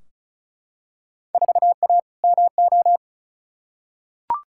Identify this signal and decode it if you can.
Morse code, heard as a keyed tone, about 36 wpm, 700 Hz. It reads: 4A MO
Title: Field Day Exchanges - 35wpm